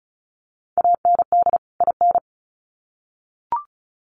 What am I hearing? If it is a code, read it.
Morse code, heard as a keyed tone, about 34 wpm, 700 Hz. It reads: AND IN
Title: Sets of 2 Words 35wpm